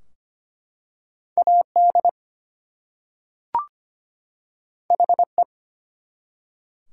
Morse code, heard as a keyed tone, about 25 wpm, 700 Hz. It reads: AD HE
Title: Two Letter Words - 25wpm